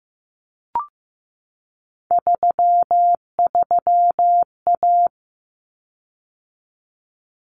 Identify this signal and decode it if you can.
Morse code, heard as a keyed tone, about 15 wpm, 700 Hz. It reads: 33A
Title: Common Cut Numbers - 3-Digit 15wpm